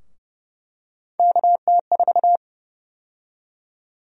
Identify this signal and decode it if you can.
Morse code, heard as a keyed tone, about 30 wpm, 700 Hz. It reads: KT4
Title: USA Call Sign Prefixes - 30wpm